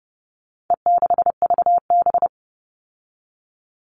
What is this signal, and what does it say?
Morse code, heard as a keyed tone, about 30 wpm, 700 Hz. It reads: E6VB
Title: All Cut Numbers - 4-Digit 30wpm